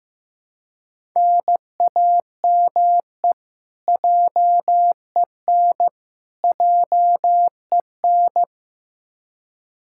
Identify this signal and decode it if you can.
Morse code, heard as a keyed tone, about 15 wpm, 700 Hz. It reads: NAME JEN JEN
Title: K1USN SST QSO Practice 15wpm